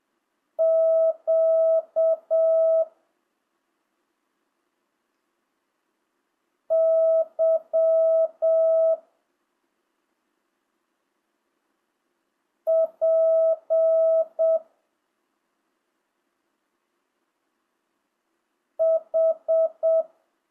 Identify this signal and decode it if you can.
Morse code, heard as a keyed tone, about 7 wpm, 650 Hz. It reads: Q Y P H
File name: jh8cbh / 2025cw